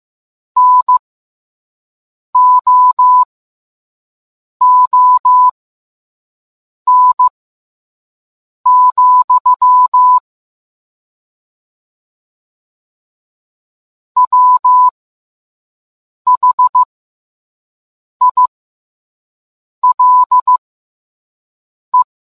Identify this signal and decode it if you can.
Morse code, heard as a keyed tone, about 15 wpm, spaced as for 5 wpm, 1000 Hz. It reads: NOON, WHILE